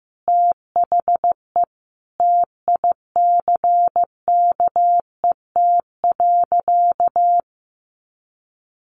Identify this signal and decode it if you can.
Morse code, heard as a keyed tone, about 15 wpm, 700 Hz. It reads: THE TICKET.
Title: Sentences from Top 1400 Words 15wpm